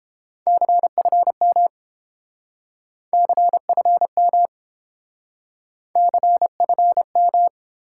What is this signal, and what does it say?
Morse code, heard as a keyed tone, about 29 wpm, 700 Hz. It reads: CFM CFM CFM